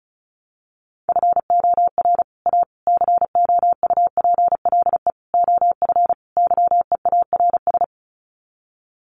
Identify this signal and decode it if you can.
Morse code, heard as a keyed tone, about 34 wpm, 700 Hz. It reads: FOR A COUPLE OF YEARS
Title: Sets of 5 Words Encore 35wpm